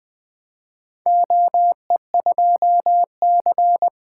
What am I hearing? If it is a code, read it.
Morse code, heard as a keyed tone, about 20 wpm, 700 Hz. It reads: OE2C
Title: Four Letter-Number Combinations - 20wpm